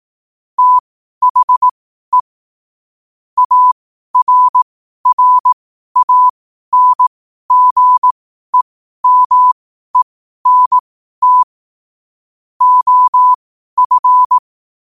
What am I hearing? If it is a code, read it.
Morse code, heard as a keyed tone, about 18 wpm, spaced as for 12 wpm, 1000 Hz. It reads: THE ARRANGEMENT OF
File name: qotdfiles / 12wpm